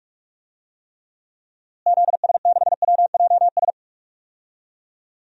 Morse code, heard as a keyed tone, about 42 wpm, 700 Hz. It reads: ZS6WJS
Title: DXCC-100 Call Signs - 6 Characters - Repeated 3x - 45wpm